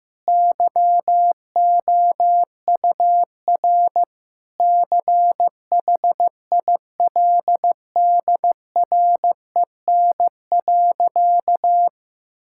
Morse code, heard as a keyed tone, about 15 wpm, 700 Hz. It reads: YOUR CHILDREN.
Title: Sentences from Top 1000 Words - Review Part 6 - 15wpm